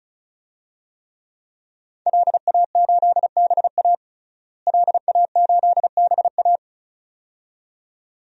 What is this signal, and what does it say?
Morse code, heard as a keyed tone, about 36 wpm, 700 Hz. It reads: LA8BA LA8BA
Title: DXCC-100 Call Signs - 5 Characters - Repeated 2x - 35wpm